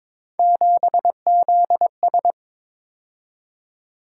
Morse code, heard as a keyed tone, about 22 wpm, 700 Hz. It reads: 7ZS
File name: practice / 22wpm / Three Letter-Number Combinations - 22wpm